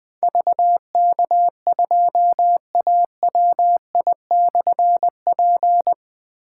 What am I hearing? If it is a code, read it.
Morse code, heard as a keyed tone, about 20 wpm, 700 Hz. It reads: VK2AWI/P
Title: Difficult Call Signs 20wpm